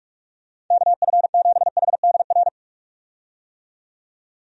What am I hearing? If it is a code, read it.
Morse code, heard as a keyed tone, about 46 wpm, 700 Hz. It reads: KF7HDR
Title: USA Call Signs - 6 Characters - 45wpm